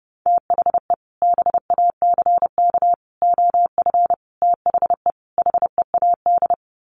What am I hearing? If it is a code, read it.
Morse code, heard as a keyed tone, about 30 wpm, 700 Hz. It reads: THE BACK OF THE HEAD
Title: Sets of 5 Words Encore2 30wpm